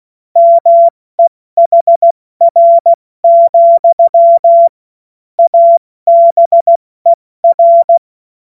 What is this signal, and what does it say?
Morse code, heard as a keyed tone, about 16 wpm, spaced as for 14 wpm, 680 Hz. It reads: MEHR, ABER
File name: Morsekurs / Witze / Witz 10 Juni Mai 2025 70 F80